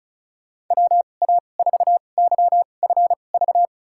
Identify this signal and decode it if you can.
Morse code, heard as a keyed tone, about 36 wpm, spaced as for 25 wpm, 700 Hz. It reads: WA4YFV
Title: USA Call Signs - 6 Characters - 2x Character Spacing - 35wpm